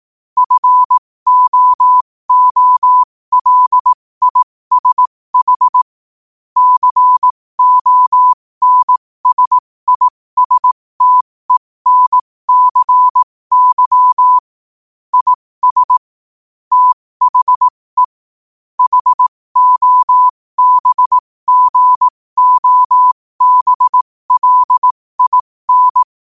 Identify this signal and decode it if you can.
Morse code, heard as a keyed tone, about 18 wpm, spaced as for 15 wpm, 1000 Hz. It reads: FOOLISH CONSISTENCY IS THE HOBGOBLIN
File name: qotdfiles / 15wpm